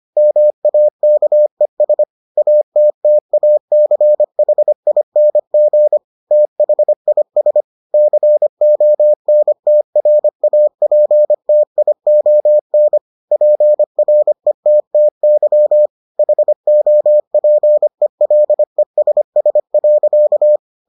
Morse code, heard as a keyed tone, about 25 wpm, 600 Hz. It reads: MAKES ATTACHING THIS CONTRAPTION PRETTY HOPELESS.